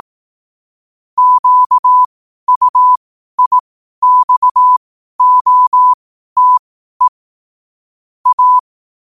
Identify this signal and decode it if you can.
Morse code, heard as a keyed tone, about 18 wpm, spaced as for 12 wpm, 1000 Hz. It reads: QUIXOTE A
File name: qotdfiles / 12wpm